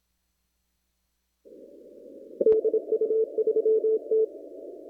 Morse code, heard as a keyed tone, about 26 wpm, 440 Hz. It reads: DU3T
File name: CW / kt5j wpx cw-op-k5pi